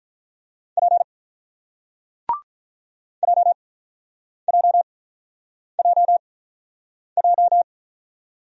Morse code, heard as a keyed tone, about 48 wpm, spaced as for 9 wpm, 700 Hz. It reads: P JJJJ